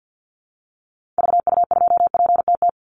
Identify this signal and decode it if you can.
Morse code, heard as a keyed tone, about 50 wpm, 700 Hz. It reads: VU2PTT
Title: Call Signs - 50wpm